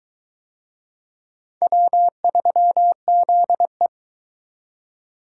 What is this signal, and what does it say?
Morse code, heard as a keyed tone, about 20 wpm, 700 Hz. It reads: W3ZE